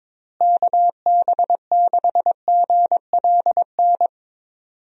Morse code, heard as a keyed tone, about 22 wpm, 700 Hz. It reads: KB6GLN
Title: USA Call Signs - 6 Characters - Repeated 3x - 22wpm